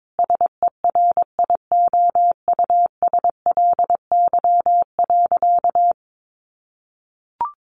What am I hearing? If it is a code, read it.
Morse code, heard as a keyed tone, about 22 wpm, 700 Hz. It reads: SERIOUSLY.
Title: Sentences from Top 1200 Words 22wpm